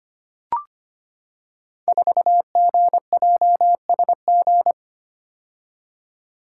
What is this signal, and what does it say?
Morse code, heard as a keyed tone, about 25 wpm, 700 Hz. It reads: 4GJSG
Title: Code Groups - 5 Letter:Numbers - 25wpm